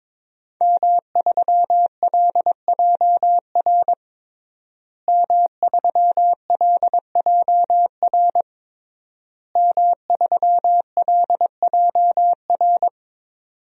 Morse code, heard as a keyed tone, about 22 wpm, 700 Hz. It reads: M3LJR M3LJR M3LJR
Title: DXCC-100 Call Signs - 5 Characters - Repeated 3x - 22wpm